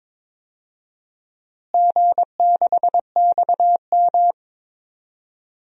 Morse code, heard as a keyed tone, about 22 wpm, 700 Hz. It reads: G6XM
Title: DXCC-100 Call Signs - 4 Characters - Repeated 3x - 22wpm